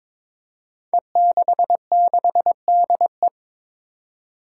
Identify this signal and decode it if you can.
Morse code, heard as a keyed tone, about 22 wpm, 700 Hz. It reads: E66DE
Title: Five Challenging Letter-Number Combinations 22wpm